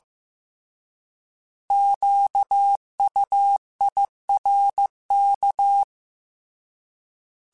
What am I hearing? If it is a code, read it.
Morse code, heard as a keyed tone, about 15 wpm, 800 Hz. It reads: QUIRK
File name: Alphabetical / Alphabetical mp3s / Qs @13